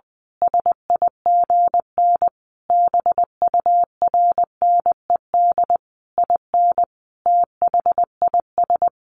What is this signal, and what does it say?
Morse code, heard as a keyed tone, about 20 wpm, 700 Hz. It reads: SIGN BURNED IN THIS
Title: A Sound Of Thunder 20wpm